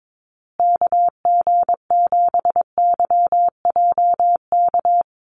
Morse code, heard as a keyed tone, about 22 wpm, 690 Hz. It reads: KG7YJK
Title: USA Call Signs Repeated 3x 22wpm